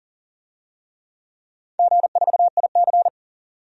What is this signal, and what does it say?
Morse code, heard as a keyed tone, about 38 wpm, 700 Hz. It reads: G4IC
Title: DXCC-100 Call Signs - 4 Characters - 40wpm